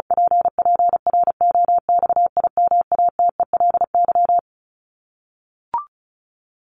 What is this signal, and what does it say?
Morse code, heard as a keyed tone, about 34 wpm, 700 Hz. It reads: PPROXIMATELY
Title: LY Suffix - Top 100 Words with Base Words 35wpm